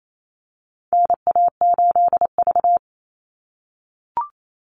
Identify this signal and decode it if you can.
Morse code, heard as a keyed tone, about 28 wpm, 700 Hz. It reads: NA8V
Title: Call Signs - 28wpm